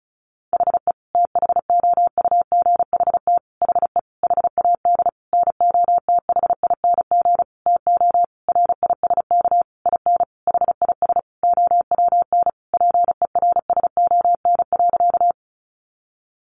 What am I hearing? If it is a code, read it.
Morse code, heard as a keyed tone, about 34 wpm, 700 Hz. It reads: HE THOUGHT HE HAD NOTHING TO RISK IN HIS OWN PERSON.
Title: Sentences from Top 600 Words 35wpm